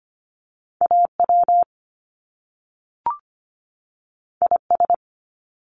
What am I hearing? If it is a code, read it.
Morse code, heard as a keyed tone, about 25 wpm, 700 Hz. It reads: AW IS
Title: Top Two Letter Combinations in Words 25wpm